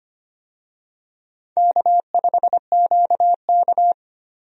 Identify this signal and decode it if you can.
Morse code, heard as a keyed tone, about 25 wpm, 700 Hz. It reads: K5QK
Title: USA Call Signs - 4 Characters - 25wpm